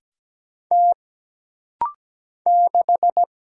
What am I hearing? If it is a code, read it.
Morse code, heard as a keyed tone, about 17 wpm, 700 Hz. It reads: T 6
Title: Single Letter-Number - Rapid-Fire 17wpm